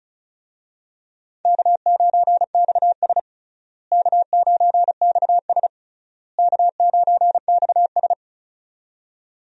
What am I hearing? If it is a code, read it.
Morse code, heard as a keyed tone, about 36 wpm, 700 Hz. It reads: K9XS K9XS K9XS
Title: USA Call Signs - 4 Characters - Repeated 3x - 35wpm